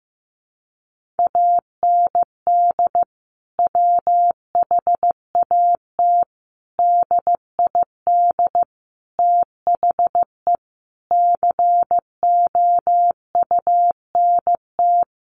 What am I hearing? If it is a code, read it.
Morse code, heard as a keyed tone, about 15 wpm, 700 Hz. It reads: AND WHAT DID THE COUNT
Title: Sentences from Top 1500 Words 15wpm